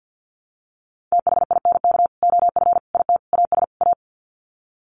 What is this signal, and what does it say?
Morse code, heard as a keyed tone, about 50 wpm, 700 Hz. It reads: THINK OF IT AS A
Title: Sets of 5 Words Encore2 50wpm